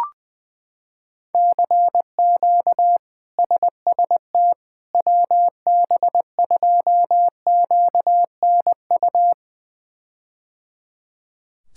Morse code, heard as a keyed tone, about 21 wpm, 700 Hz. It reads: CQ SST WB2QNU
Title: K1USN SST QSO Practice - 20wpm